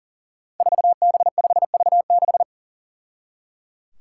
Morse code, heard as a keyed tone, about 38 wpm, 700 Hz. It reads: 4B5V6
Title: Five Challenging Letter-Number Combinations 40wpm